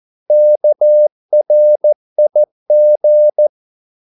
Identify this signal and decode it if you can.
Morse code, heard as a keyed tone, about 14 wpm, 600 Hz. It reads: KRIG
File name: Korte ord DK 14wpm